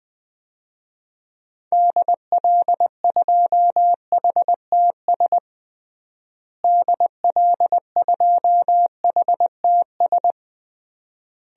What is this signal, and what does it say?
Morse code, heard as a keyed tone, about 20 wpm, 700 Hz. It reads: DL2HTS DL2HTS
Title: DXCC-100 Call Signs - 6 Characters - Repeated 3x - 20wpm